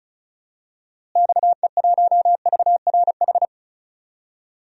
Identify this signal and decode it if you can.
Morse code, heard as a keyed tone, about 36 wpm, 700 Hz. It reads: XE1VRH
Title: DXCC-100 Call Signs - 6 Characters - Repeated 3x - 35wpm